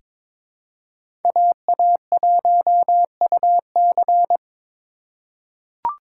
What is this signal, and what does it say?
Morse code, heard as a keyed tone, about 22 wpm, 700 Hz. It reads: AA1UC
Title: K1USN SST QSO Practice - 22wpm